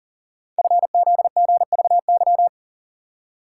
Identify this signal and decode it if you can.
Morse code, heard as a keyed tone, about 42 wpm, 700 Hz. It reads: FZGVY
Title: Code Groups - 5 Letters - 40wpm